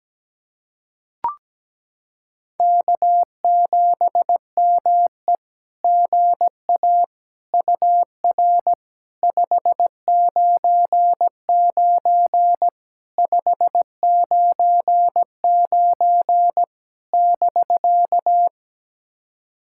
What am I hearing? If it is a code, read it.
Morse code, heard as a keyed tone, about 17 wpm, 700 Hz. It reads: K7ME GA UR 599 599 <BK>
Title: SOTA QSO Practice 17wpm